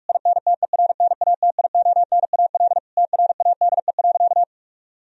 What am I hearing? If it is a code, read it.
Morse code, heard as a keyed tone, about 41 wpm, 700 Hz. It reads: INTERNATIONAL TRADE.
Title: Sentences from Top 1000 Words - Review Part 4 - 45wpm